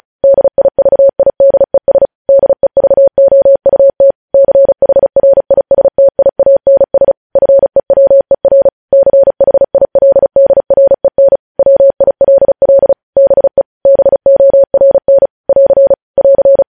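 Morse code, heard as a keyed tone, about 36 wpm, 550 Hz. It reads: DIVIDES DEVOUT CHRISTIANS FEWER CHILDREN WILL BE BORN <AR> <AR>
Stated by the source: Morse code news headlines practice sent at 35 wpm.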